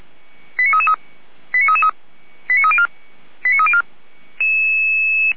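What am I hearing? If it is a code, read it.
Five-tone selective call (ZVEI), heard as quick runs of five tones, tones of about 70 ms. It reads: 89333 89333 89304 89304
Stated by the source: ZVEI five-tone selective call received on an SDR.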